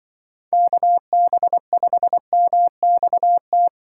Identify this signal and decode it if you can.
Morse code, heard as a keyed tone, about 25 wpm, 700 Hz. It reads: KB5MXT
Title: Speed Racing - USA Call Signs 20wpm